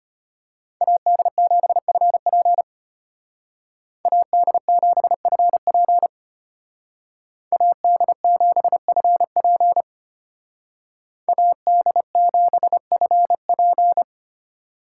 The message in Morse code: AD7FP AD7FP AD7FP AD7FP

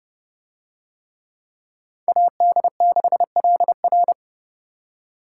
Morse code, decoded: AD6LR

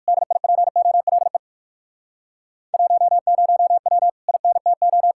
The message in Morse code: DIPOLE 10W INTO

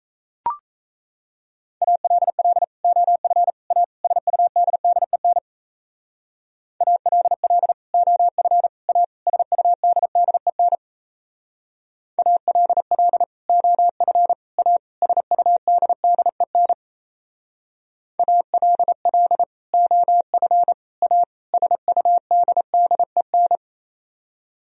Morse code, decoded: ALL OF A SUDDEN ALL OF A SUDDEN ALL OF A SUDDEN ALL OF A SUDDEN

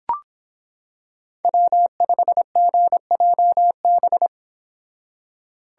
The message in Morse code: W5GJB

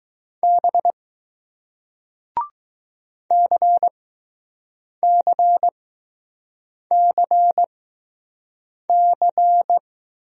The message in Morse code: B CCCC

